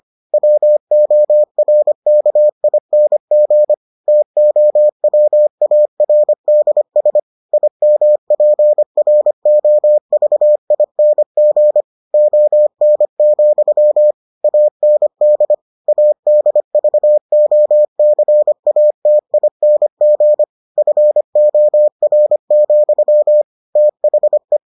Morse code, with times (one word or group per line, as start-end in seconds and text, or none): 0.3-3.7 WORKING
4.1-7.2 TOWARDS
7.5-11.8 IMPROVING
12.1-14.1 ON,
14.4-15.5 AND
15.9-20.4 ADVOCATING
20.8-23.4 FOR,
23.8-24.6 THE